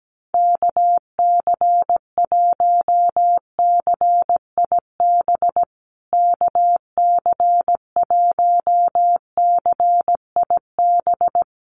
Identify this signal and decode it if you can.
Morse code, heard as a keyed tone, about 17 wpm, 700 Hz. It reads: KC1CIB KC1CIB
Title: USA Call Signs Repeated 3x 17wpm